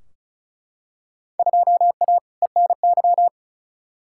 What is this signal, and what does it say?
Morse code, heard as a keyed tone, about 36 wpm, 700 Hz. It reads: 2A ENY
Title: Field Day Exchanges Encore - 35wpm